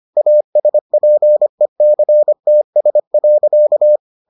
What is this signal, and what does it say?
Morse code, heard as a keyed tone, about 25 wpm, 600 Hz. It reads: ASPECTS.